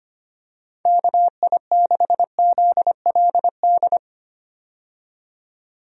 Morse code, heard as a keyed tone, about 25 wpm, 700 Hz. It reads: KI6ZLD
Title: USA Call Signs - 6 Characters - 25wpm